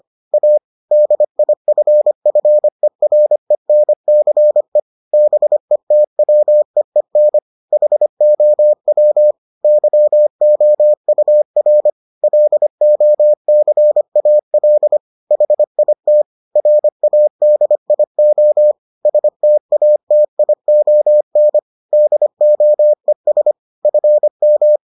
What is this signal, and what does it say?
Morse code, heard as a keyed tone, about 25 wpm, 600 Hz. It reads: A DIFFERENCE BETWEEN HOW YOUR LOCAL HIT RADIO STATION DOES FM